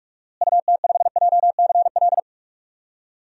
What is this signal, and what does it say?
Morse code, heard as a keyed tone, about 46 wpm, 700 Hz. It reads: UT5JCL
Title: DXCC-100 Call Signs - 6 Characters - Repeated 2x - 45wpm